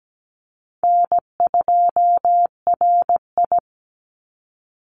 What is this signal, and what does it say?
Morse code, heard as a keyed tone, about 17 wpm, 700 Hz. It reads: N2RI
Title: Easy Call Signs Repeated 3x - 17wpm